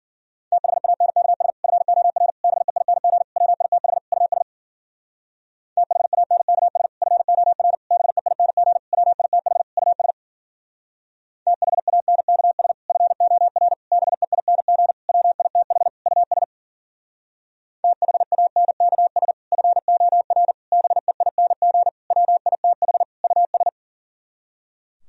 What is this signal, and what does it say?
Morse code, heard as a keyed tone, about 51 wpm, 700 Hz. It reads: THANKS FOR BEING WITH US THANKS FOR BEING WITH US THANKS FOR BEING WITH US THANKS FOR BEING WITH US